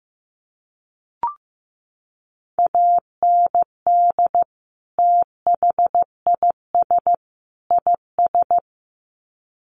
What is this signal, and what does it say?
Morse code, heard as a keyed tone, about 15 wpm, 700 Hz. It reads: AND THIS IS
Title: Sets of 3 Words 15wpm